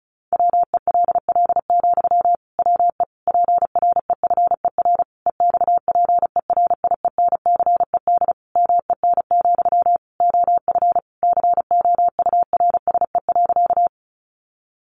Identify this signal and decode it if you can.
Morse code, heard as a keyed tone, about 34 wpm, 700 Hz. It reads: WELL, WE PREFER EXPERIENCED MEN, OF COURSE.
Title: Sentences from Top 1100 Words 35wpm